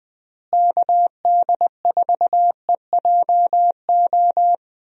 Morse code, heard as a keyed tone, about 20 wpm, 700 Hz. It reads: KD4EJO
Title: USA Call Signs - 6 Characters - Repeated 3x - 20wpm